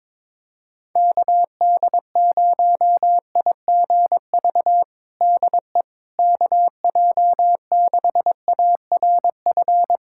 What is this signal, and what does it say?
Morse code, heard as a keyed tone, about 22 wpm, 700 Hz. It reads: KD0IGV DE KJ6ARF
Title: USA Callsign DE Callsign - 6 Characters - 22wpm